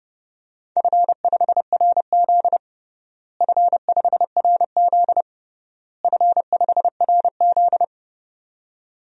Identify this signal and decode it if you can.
Morse code, heard as a keyed tone, about 31 wpm, 700 Hz. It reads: F5RZ F5RZ F5RZ
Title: DXCC-100 Call Signs - 4 Characters - Repeated 3x - 30wpm